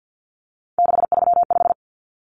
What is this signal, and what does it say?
Morse code, heard as a keyed tone, about 50 wpm, 700 Hz. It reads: 635
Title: Common Cut Numbers - 3-Digit 50wpm